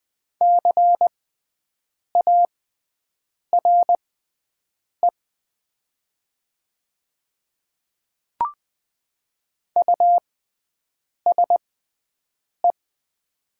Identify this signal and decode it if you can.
Morse code, heard as a keyed tone, about 20 wpm, spaced as for 7 wpm, 700 Hz. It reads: CARE USE
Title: 6x Character Spacing - Top 500 Words - 20wpm